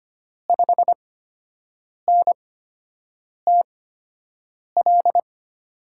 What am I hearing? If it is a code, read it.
Morse code, heard as a keyed tone, about 25 wpm, spaced as for 7 wpm, 700 Hz. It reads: 5NTL
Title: USA Call Signs - 6 Characters - 8x Character Spacing - 25wpm